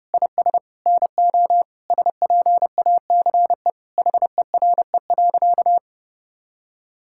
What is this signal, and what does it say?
Morse code, heard as a keyed tone, about 30 wpm, 700 Hz. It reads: IS NO SPACE HERE.
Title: Sentences from Top 1000 Words - Review Part 5 - 30wpm